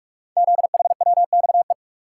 Morse code, heard as a keyed tone, about 44 wpm, 690 Hz. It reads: ZHWXE
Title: Code Groups - 5 Letters - 45wpm